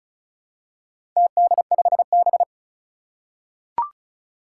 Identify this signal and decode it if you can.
Morse code, heard as a keyed tone, about 35 wpm, 700 Hz. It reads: TD5B
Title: Four Challenging Letter-Number Combinations 35wpm